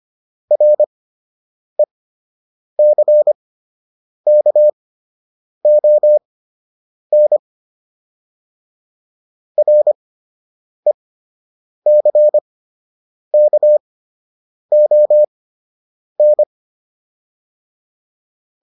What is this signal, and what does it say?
Morse code, heard as a keyed tone, about 25 wpm, spaced as for 8 wpm, 600 Hz. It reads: RECKON RECKON